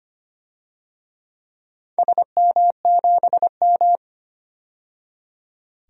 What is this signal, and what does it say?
Morse code, heard as a keyed tone, about 25 wpm, 700 Hz. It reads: SM7M
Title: DXCC-100 Call Signs - 4 Characters - 25wpm